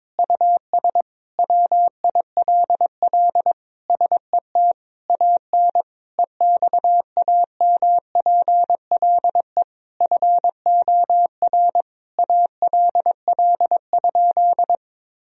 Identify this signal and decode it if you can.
Morse code, heard as a keyed tone, about 22 wpm, 700 Hz. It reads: US WILL SET AN EXAMPLE FOR ALL?
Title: Sentences from Top 1000 Words - Review Part 3 - 22wpm